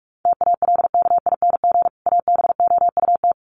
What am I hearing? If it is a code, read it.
Morse code, heard as a keyed tone, about 44 wpm, 700 Hz. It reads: TALKING ABOUT
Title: Sets of 2 Words 45wpm